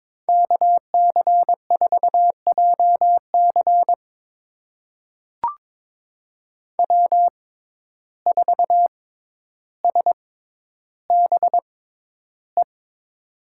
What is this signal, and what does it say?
Morse code, heard as a keyed tone, about 22 wpm, spaced as for 11 wpm, 700 Hz. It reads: KC4JC W4SBE
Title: USA Call Signs - 5 Characters - 6x Character Spacing - 22wpm